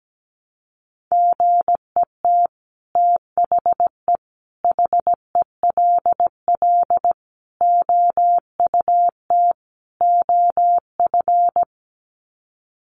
Morse code, heard as a keyed tone, about 17 wpm, 700 Hz. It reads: GET THE HELL OUT OF
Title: Sets of 5 Words Encore2 17wpm